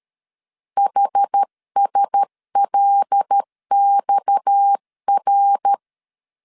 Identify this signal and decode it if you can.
Morse code, heard as a keyed tone, about 13 wpm, 800 Hz. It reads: HSLXR
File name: CW / DET / 1 6 1500C S C 12wpm 3533